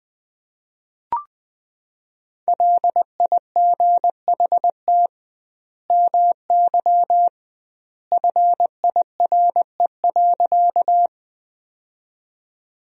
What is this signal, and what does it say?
Morse code, heard as a keyed tone, about 20 wpm, 700 Hz. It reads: LIGHT MY FIRE.
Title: 2x Word-Spacing - Sentences from Top 400 Words - 20wpm